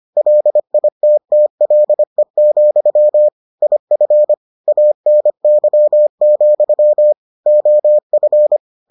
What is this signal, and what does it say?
Morse code, heard as a keyed tone, about 25 wpm, 600 Hz. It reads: LITTLE, IF ANY, OF